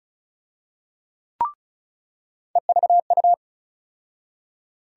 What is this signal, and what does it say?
Morse code, heard as a keyed tone, about 36 wpm, 700 Hz. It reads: EVU
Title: USA Call Sign Postfixes - 3 Letters 35wpm